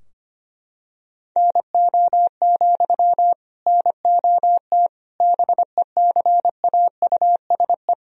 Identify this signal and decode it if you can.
Morse code, heard as a keyed tone, about 25 wpm, 700 Hz. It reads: NO, NOT BECAUSE
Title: Sentences from Top 1000 Words - Review Part 3 - 25wpm